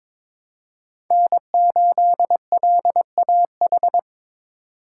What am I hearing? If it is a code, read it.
Morse code, heard as a keyed tone, about 22 wpm, 700 Hz. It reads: N8LAH
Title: USA Call Signs - 5 Characters - 4x Character Spacing - 22wpm